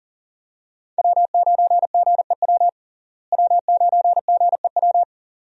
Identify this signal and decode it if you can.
Morse code, heard as a keyed tone, about 40 wpm, 700 Hz. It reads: W9GEW W9GEW
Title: USA Call Signs - 5 Characters - Repeated 2x - 40wpm